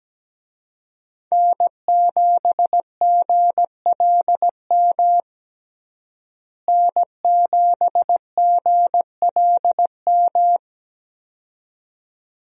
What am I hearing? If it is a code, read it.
Morse code, heard as a keyed tone, about 17 wpm, 700 Hz. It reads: N7GLM N7GLM
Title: USA Call Signs - 5 Characters - Repeated 2x - 17wpm